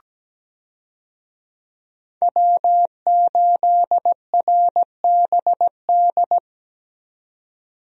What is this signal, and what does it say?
Morse code, heard as a keyed tone, about 17 wpm, 700 Hz. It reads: W8RBD
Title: USA Call Signs - 5 Characters - 17wpm